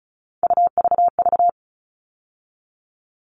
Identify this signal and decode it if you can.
Morse code, heard as a keyed tone, about 34 wpm, 700 Hz. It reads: UVV
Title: All Cut Numbers - 3-Digit 35wpm